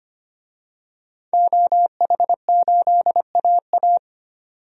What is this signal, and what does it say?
Morse code, heard as a keyed tone, about 25 wpm, 700 Hz. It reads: OH8AA
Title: DXCC-100 Call Signs - 5 Characters - 25wpm